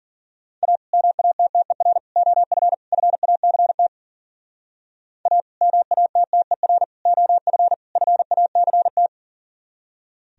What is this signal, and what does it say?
Morse code, heard as a keyed tone, about 49 wpm, 700 Hz. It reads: A MATTER OF FACT A MATTER OF FACT
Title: Speed-Racing - Sets of 4 Words 40wpm